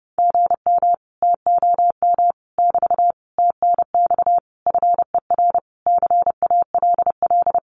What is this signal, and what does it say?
Morse code, heard as a keyed tone, about 30 wpm, 700 Hz. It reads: GM TOM <BT> TNX FER CALL
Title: Common QSO Phrases Encore 30wpm